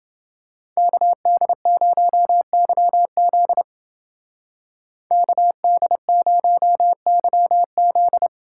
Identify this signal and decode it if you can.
Morse code, heard as a keyed tone, about 29 wpm, 700 Hz. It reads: KD0YZ KD0YZ